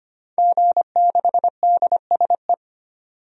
Morse code, heard as a keyed tone, about 25 wpm, 700 Hz. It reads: G6DSE
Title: DXCC-100 Call Signs - 5 Characters - 25wpm